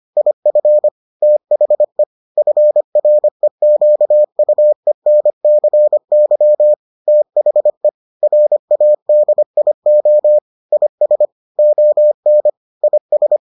Morse code, heard as a keyed tone, about 25 wpm, 600 Hz. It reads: IF THE FREQUENCY THE RADIO IS ON IS